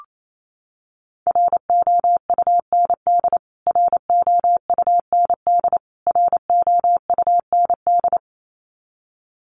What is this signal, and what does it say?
Morse code, heard as a keyed tone, about 28 wpm, 700 Hz. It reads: ROUND ROUND ROUND
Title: Top 200 Words Repeated 28wpm